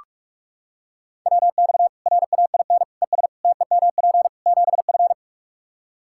Morse code, heard as a keyed tone, about 46 wpm, 700 Hz. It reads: WX RAIN ES TEMP 7F
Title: Practice QSO - Exchange 2 - Info - 45wpm